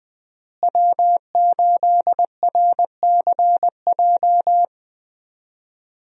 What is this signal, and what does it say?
Morse code, heard as a keyed tone, about 20 wpm, 700 Hz. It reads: W8RCJ
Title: USA Call Signs - 5 Characters - Repeated 3x - 20wpm